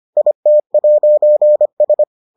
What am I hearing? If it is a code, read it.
Morse code, heard as a keyed tone, about 25 wpm, 600 Hz. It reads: IT'S